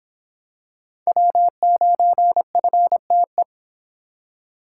Morse code, heard as a keyed tone, about 24 wpm, 700 Hz. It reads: W9FTE